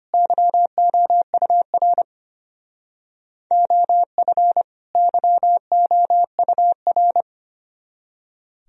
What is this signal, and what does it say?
Morse code, heard as a keyed tone, about 30 wpm, 700 Hz. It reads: YOUR OF YOUR
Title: Speed-Racing - Sets of 2 Words 25wpm